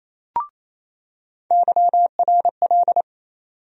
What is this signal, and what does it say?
Morse code, heard as a keyed tone, about 28 wpm, 700 Hz. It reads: YRL
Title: USA Call Sign Postfixes - 3 Letters 28wpm